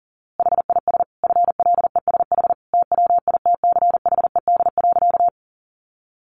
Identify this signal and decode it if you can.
Morse code, heard as a keyed tone, about 37 wpm, 700 Hz. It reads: HIS FLESH TWITCHED.
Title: A Sound Of Thunder 40wpm